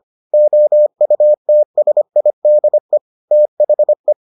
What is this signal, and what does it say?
Morse code, heard as a keyed tone, about 25 wpm, 600 Hz. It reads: OUTSIDE THE